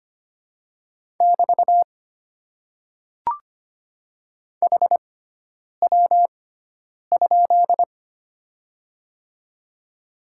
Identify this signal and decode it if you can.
Morse code, heard as a keyed tone, about 25 wpm, spaced as for 9 wpm, 690 Hz. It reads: <BT> HW?
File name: Top 90 QSO Elements - 6x Character Spacing - 25wpm